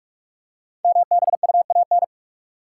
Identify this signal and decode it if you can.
Morse code, heard as a keyed tone, about 44 wpm, 700 Hz. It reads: MBUAN
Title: Code Groups - 5 Letters - 45wpm